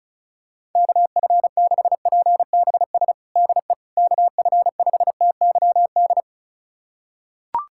Morse code, heard as a keyed tone, about 34 wpm, 700 Hz. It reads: KF6PBS DE KF5TYD
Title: USA Callsign DE Callsign - 6 Characters - 35wpm